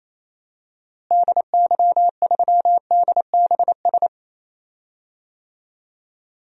Morse code, heard as a keyed tone, about 28 wpm, 700 Hz. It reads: DY3DBS
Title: DXCC-100 Call Signs - 6 Characters - Repeated 2x - 28wpm